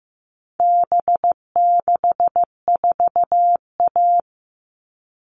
Morse code, heard as a keyed tone, about 15 wpm, 700 Hz. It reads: B64A
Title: All Cut Numbers - 4-Digit 15wpm